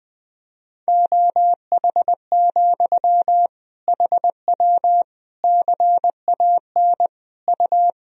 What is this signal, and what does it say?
Morse code, heard as a keyed tone, about 20 wpm, 700 Hz. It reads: OH, HW CAN U
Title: Sentences from Top 500 Words and Abbreviations 20wpm